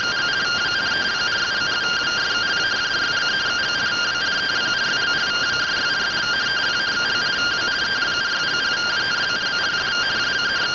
Two-tone teleprinter signal, about 170 Hz shift.